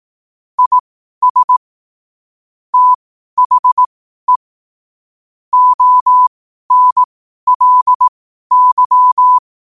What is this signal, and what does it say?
Morse code, heard as a keyed tone, about 18 wpm, spaced as for 12 wpm, 1000 Hz. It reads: IS THE ONLY
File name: qotdfiles / 12wpm